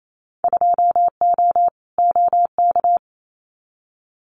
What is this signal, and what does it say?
Morse code, heard as a keyed tone, about 28 wpm, 700 Hz. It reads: 2O OK
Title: Winter Field Day Exchanges 28wpm